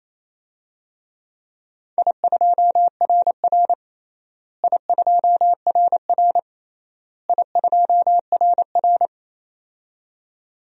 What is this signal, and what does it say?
Morse code, heard as a keyed tone, about 28 wpm, 700 Hz. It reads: I2RR I2RR I2RR
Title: DXCC-100 Call Signs - 4 Characters - Repeated 3x - 28wpm